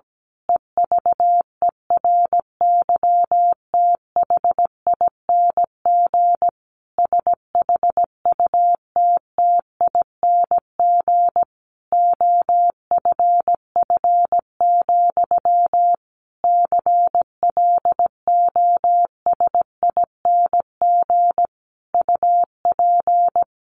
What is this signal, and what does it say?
Morse code, heard as a keyed tone, about 17 wpm, 700 Hz. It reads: EVERYTHING SHUTTING OFF, CLOSING UP
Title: A Sound Of Thunder 17wpm